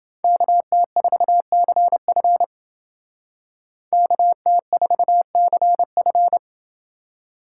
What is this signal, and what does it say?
Morse code, heard as a keyed tone, about 31 wpm, 700 Hz. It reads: KT4CF KT4CF